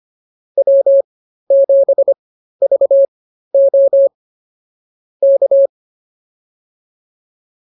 Morse code, heard as a keyed tone, about 24 wpm, spaced as for 13 wpm, 550 Hz. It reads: W7VO K